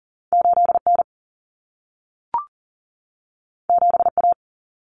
Morse code, heard as a keyed tone, about 40 wpm, 700 Hz. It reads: 8N 7A
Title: Common Cut Numbers - 2-Digit 40wpm